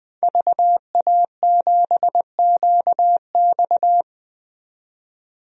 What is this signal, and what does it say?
Morse code, heard as a keyed tone, about 20 wpm, 700 Hz. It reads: VA7QX
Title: DXCC-100 Call Signs - 5 Characters - 20wpm